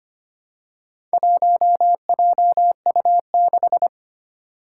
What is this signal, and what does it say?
Morse code, heard as a keyed tone, about 25 wpm, 700 Hz. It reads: 1JU6
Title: Four Challenging Letter-Number Combinations 25wpm